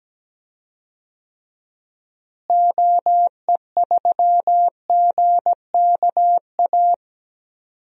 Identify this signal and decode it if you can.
Morse code, heard as a keyed tone, about 17 wpm, 700 Hz. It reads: OE3GKA
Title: DXCC-100 Call Signs - 6 Characters - Repeated 2x - 17wpm